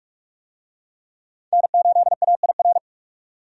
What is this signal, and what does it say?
Morse code, heard as a keyed tone, about 42 wpm, 700 Hz. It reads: N8AIR
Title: USA Call Signs - 5 Characters - 45wpm